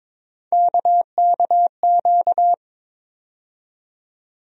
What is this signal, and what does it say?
Morse code, heard as a keyed tone, about 21 wpm, 700 Hz. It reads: KKQ
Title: USA Call Sign Postfixes - 3 Letters 22wpm